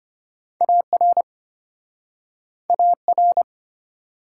Morse code, heard as a keyed tone, about 34 wpm, 700 Hz. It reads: AR AR